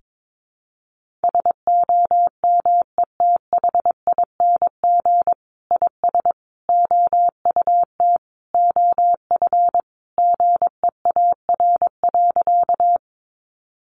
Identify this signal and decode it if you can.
Morse code, heard as a keyed tone, about 22 wpm, 700 Hz. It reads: SOMETHING IS OUT OF GEAR.
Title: Sentences from Top 1400 Words 22wpm